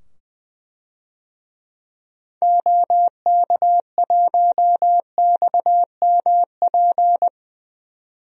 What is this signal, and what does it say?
Morse code, heard as a keyed tone, about 20 wpm, 700 Hz. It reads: OK1XMP
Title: DXCC-100 Call Signs - 6 Characters - 20wpm